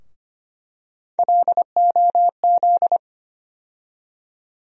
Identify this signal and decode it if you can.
Morse code, heard as a keyed tone, about 25 wpm, 700 Hz. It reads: LOZ
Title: USA Call Sign Postfixes - 3 Letters 25wpm